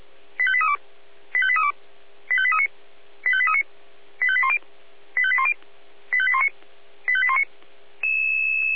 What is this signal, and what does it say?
Five-tone selective call (ZVEI), heard as quick runs of five tones, tones of about 70 ms. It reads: 85832 85832 85839 85839 85810 85810 85819 85819